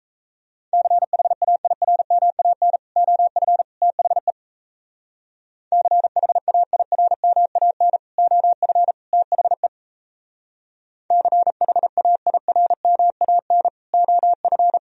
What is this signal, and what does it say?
Morse code, heard as a keyed tone, about 36 wpm, 700 Hz. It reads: CHAIRMAN OF THE CHAIRMAN OF THE CHAIRMAN OF